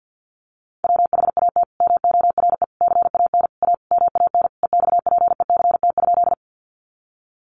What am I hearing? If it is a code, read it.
Morse code, heard as a keyed tone, about 50 wpm, 700 Hz. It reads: WHAT MORE CAN A MAN EXPECT?
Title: Sentences from Top 1100 Words 50wpm